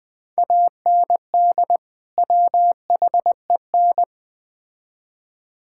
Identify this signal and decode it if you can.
Morse code, heard as a keyed tone, about 21 wpm, 700 Hz. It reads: AND WHEN
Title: Build A Phrase - 5-Word Phrases - 20wpm